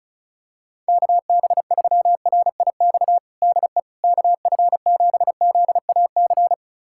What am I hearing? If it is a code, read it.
Morse code, heard as a keyed tone, about 36 wpm, 700 Hz. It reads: KB3RIX DE KF7ZAC
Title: USA Callsign DE Callsign - 6 Characters - 35wpm